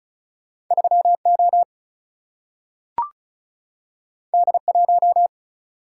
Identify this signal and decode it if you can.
Morse code, heard as a keyed tone, about 36 wpm, 700 Hz. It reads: 3 O D 1
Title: Single-Letter Single-Number 35wpm